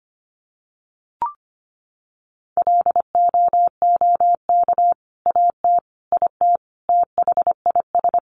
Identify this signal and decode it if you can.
Morse code, heard as a keyed tone, about 25 wpm, 700 Hz. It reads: LOOK AT IT THIS
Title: Sentences from Top 100 Words 25wpm